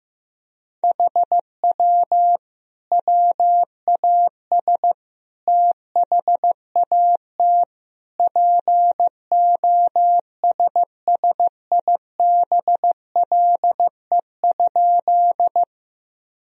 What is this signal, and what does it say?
Morse code, heard as a keyed tone, about 15 wpm, 700 Hz. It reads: HW WAS THAT POSSIBLE?
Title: Sentences from Top 500 Words and Abbreviations 15wpm